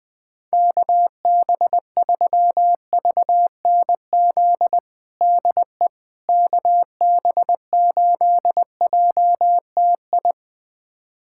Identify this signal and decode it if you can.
Morse code, heard as a keyed tone, about 21 wpm, 700 Hz. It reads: KB3VNZ DE KB8JTI
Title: USA Callsign DE Callsign 20wpm